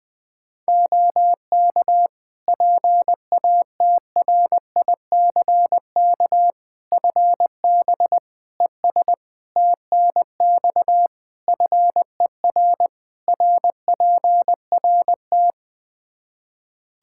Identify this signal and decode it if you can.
Morse code, heard as a keyed tone, about 21 wpm, 700 Hz. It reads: OK PATRICK FB ES TNX FER RPRT
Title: Practice QSO - Exchange 2 - Info - 20wpm